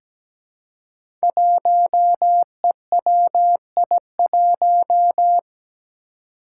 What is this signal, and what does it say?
Morse code, heard as a keyed tone, about 17 wpm, 700 Hz. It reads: 1EWI1
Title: Five Challenging Letter-Number Combinations 17wpm